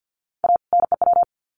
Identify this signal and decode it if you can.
Morse code, heard as a keyed tone, about 50 wpm, 700 Hz. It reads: A NEW
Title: Sets of 2 Words 50wpm